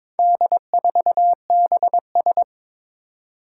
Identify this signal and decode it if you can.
Morse code, heard as a keyed tone, about 22 wpm, 700 Hz. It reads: D4BS
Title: Four Challenging Letter-Number Combinations 22wpm